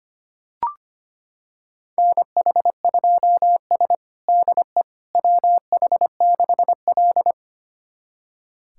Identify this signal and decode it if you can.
Morse code, heard as a keyed tone, about 25 wpm, 700 Hz. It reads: NH2S DE WH6L
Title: USA Callsign DE Callsign - 4 Characters - 25wpm